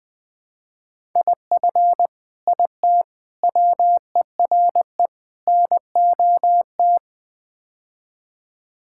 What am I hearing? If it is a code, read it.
Morse code, heard as a keyed tone, about 20 wpm, 700 Hz. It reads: IF IT WERE NOT
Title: Build A Phrase - 5-Word Phrases - 20wpm